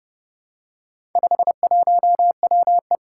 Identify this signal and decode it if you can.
Morse code, heard as a keyed tone, about 30 wpm, 700 Hz. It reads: 51WE
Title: Four Challenging Letter-Number Combinations 30wpm